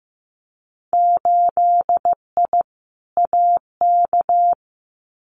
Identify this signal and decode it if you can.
Morse code, heard as a keyed tone, about 15 wpm, 700 Hz. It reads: 8I AK
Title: Winter Field Day Exchanges Encore 15wpm